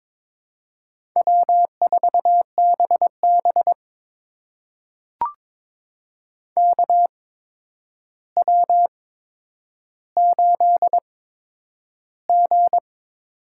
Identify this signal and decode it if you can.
Morse code, heard as a keyed tone, about 22 wpm, spaced as for 9 wpm, 700 Hz. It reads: W4BB KW8G
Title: USA Call Signs - 4 Characters - 8x Character Spacing - 22wpm